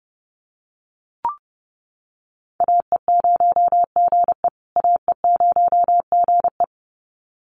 Morse code, heard as a keyed tone, about 29 wpm, 700 Hz. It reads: AE0GE AE0GE
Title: USA Call Signs Repeated 2x 30wpm